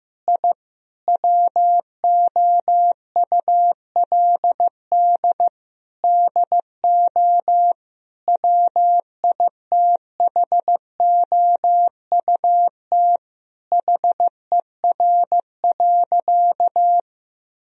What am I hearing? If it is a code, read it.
Morse code, heard as a keyed tone, about 15 wpm, 700 Hz. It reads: I WOULD DO WITHOUT HER.
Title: Sentences from Top 1000 Words - Review Part 3 - 15wpm